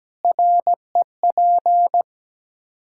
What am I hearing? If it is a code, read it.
Morse code, heard as a keyed tone, about 17 wpm, 700 Hz. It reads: REP
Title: Practice QSO - Exchange 3 - More Info - 17wpm